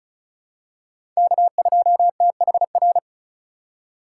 Morse code, heard as a keyed tone, about 34 wpm, 700 Hz. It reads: K2THR
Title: USA Call Signs - 5 Characters - 4x Character Spacing - 35wpm